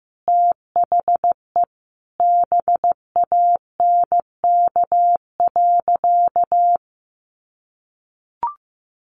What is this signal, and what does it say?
Morse code, heard as a keyed tone, about 14 wpm, 700 Hz. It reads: THE BANK.
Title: Sentences from Top 1500 Words 15wpm